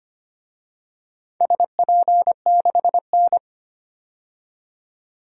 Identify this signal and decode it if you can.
Morse code, heard as a keyed tone, about 25 wpm, 700 Hz. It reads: SP6N
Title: DXCC-100 Call Signs - 4 Characters - 25wpm